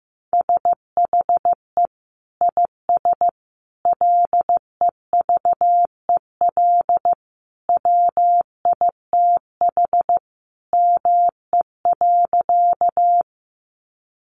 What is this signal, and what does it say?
Morse code, heard as a keyed tone, about 15 wpm, 700 Hz. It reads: SHE IS LEVEL WITH ME.
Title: Sentences from Top 700 Words 15wpm